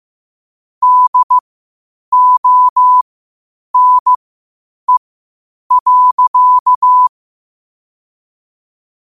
Morse code, heard as a keyed tone, about 15 wpm, spaced as for 8 wpm, 1000 Hz. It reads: DONE.
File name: qotdfiles / 8wpm